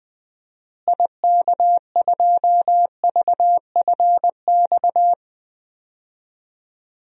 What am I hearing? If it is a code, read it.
Morse code, heard as a keyed tone, about 20 wpm, 700 Hz. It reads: IK2VFX
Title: DXCC-100 Call Signs - 6 Characters - Repeated 2x - 20wpm